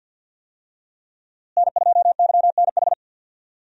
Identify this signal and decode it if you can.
Morse code, heard as a keyed tone, about 51 wpm, 690 Hz. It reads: N2YNH
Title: USA Call Signs - 5 Characters - Repeated 3x - 50wpm